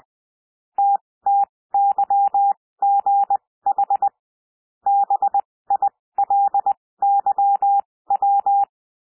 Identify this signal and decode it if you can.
Morse code, heard as a keyed tone, about 20 wpm, spaced as for 16 wpm, 800 Hz. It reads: TTYGH BILYW
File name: cwmp3 / L1 SOUND LETTERS 20 wpm